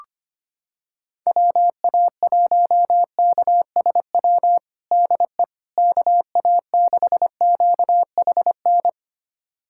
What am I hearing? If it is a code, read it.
Morse code, heard as a keyed tone, about 25 wpm, 700 Hz. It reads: WA1KSW DE KA6QHN
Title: USA Callsign DE Callsign - 6 Characters - 25wpm